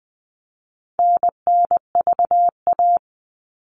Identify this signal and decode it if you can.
Morse code, heard as a keyed tone, about 20 wpm, 700 Hz. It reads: NNVA
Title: All Cut Numbers - 4-Digit 20wpm